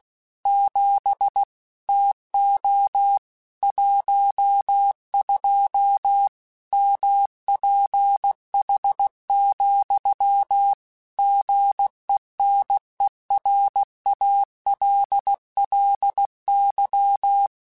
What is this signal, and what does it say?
Morse code, heard as a keyed tone, about 16 wpm, 800 Hz. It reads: 7 TO 12 MPH, GENERALLY